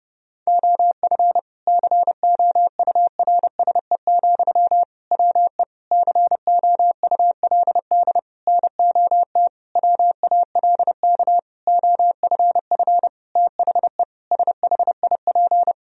Morse code, heard as a keyed tone, about 30 wpm, 700 Hz. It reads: OF COURSE, WE COULD NOT WALK OFF THE SHIP
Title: Sentences from Top 1000 Words - Review Part 2 - 30wpm